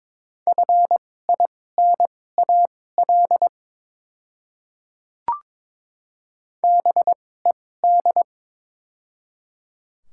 Morse code, heard as a keyed tone, about 22 wpm, spaced as for 16 wpm, 700 Hz. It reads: FINAL BED
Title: 2x Character Spacing - Top 500 Words - 22wpm